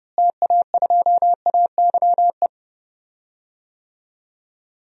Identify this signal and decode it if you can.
Morse code, heard as a keyed tone, about 32 wpm, 700 Hz. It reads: TA2AYE
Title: DXCC-100 Call Signs - 6 Characters - Repeated 2x - 30wpm